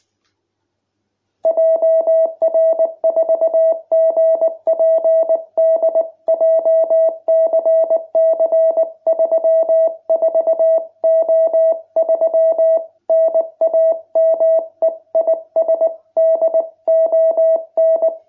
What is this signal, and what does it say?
Morse code, heard as a keyed tone, about 19 wpm, 660 Hz. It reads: JR4GPD JCC34O3 NAME IS DON